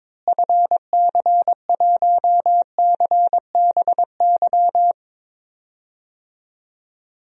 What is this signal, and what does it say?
Morse code, heard as a keyed tone, about 22 wpm, 700 Hz. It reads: FC1CBY
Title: DXCC-100 Call Signs - 6 Characters - Repeated 2x - 22wpm